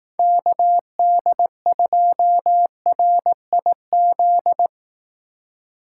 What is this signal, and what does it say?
Morse code, heard as a keyed tone, about 18 wpm, 700 Hz. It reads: KD2RIZ
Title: Speed Racing - USA 2x3 Call Signs 15wpm